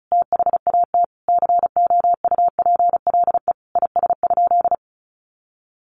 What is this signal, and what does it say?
Morse code, heard as a keyed tone, about 34 wpm, 700 Hz. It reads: THAT COUPLE IS?
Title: Sentences from Top 1100 Words 35wpm